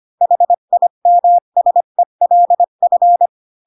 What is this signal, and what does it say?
Morse code, heard as a keyed tone, about 25 wpm, 700 Hz. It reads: HIMSELF